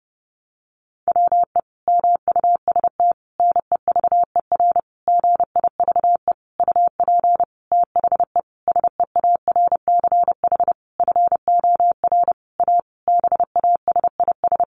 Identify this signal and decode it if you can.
Morse code, heard as a keyed tone, about 30 wpm, 700 Hz. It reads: WE MUST NEVER GIVE UP THE SEARCH FOR A BASIS
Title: Sentences from Top 1500 Words 30wpm